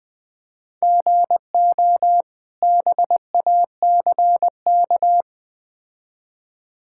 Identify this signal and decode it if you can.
Morse code, heard as a keyed tone, about 20 wpm, 700 Hz. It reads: GO BACK
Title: Build A Phrase - 4-Word Phrases - 20wpm